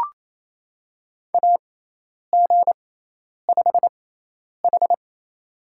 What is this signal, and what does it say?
Morse code, heard as a keyed tone, about 28 wpm, spaced as for 10 wpm, 700 Hz. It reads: AG5H
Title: USA Call Signs - 6x Character Spacing - 28wpm